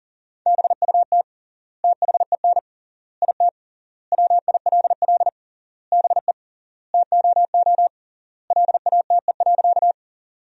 Morse code, heard as a keyed tone, about 40 wpm, 700 Hz. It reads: BUT THEN IT WILL BE TOO LATE.
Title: 3x Word-Spacing - Sentences from Top 300 Words - 40wpm